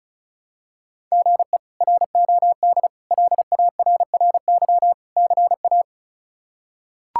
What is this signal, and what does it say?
Morse code, heard as a keyed tone, about 34 wpm, 700 Hz. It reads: GE ROD LARRY CA
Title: K1USN SST QSO Practice - 35wpm